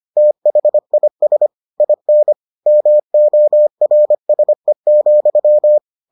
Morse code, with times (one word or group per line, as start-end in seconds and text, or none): 0.2-1.5 THIS
1.8-2.3 IN
2.7-5.8 MORSE,